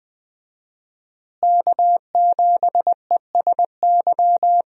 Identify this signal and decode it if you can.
Morse code, heard as a keyed tone, about 20 wpm, 700 Hz. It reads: K7ESY
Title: USA Call Signs - 5 Characters - Repeated 2x - 20wpm